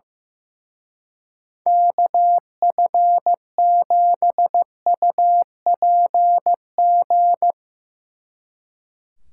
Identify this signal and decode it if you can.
Morse code, heard as a keyed tone, about 16 wpm, 700 Hz. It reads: KF7UPG
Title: Speed Racing - USA 2x3 Call Signs 15wpm